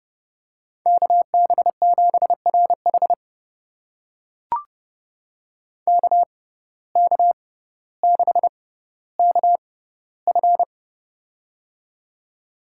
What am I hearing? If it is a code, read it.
Morse code, heard as a keyed tone, about 30 wpm, spaced as for 15 wpm, 700 Hz. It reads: KB7RH KK6KF
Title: USA Call Signs - 6x Character Spacing - 30wpm